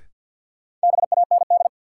Morse code, heard as a keyed tone, about 53 wpm, 700 Hz. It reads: BAND
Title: Sentences from Top 1600 Words - New Word Drill - 50wpm